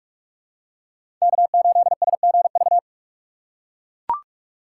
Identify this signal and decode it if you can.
Morse code, heard as a keyed tone, about 42 wpm, 700 Hz. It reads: K8SGV